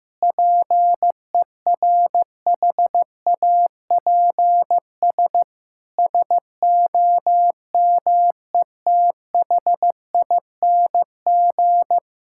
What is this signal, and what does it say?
Morse code, heard as a keyed tone, about 15 wpm, 700 Hz. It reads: PERHAPS SOMETHING
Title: Sentences from Top 1600 Words 15wpm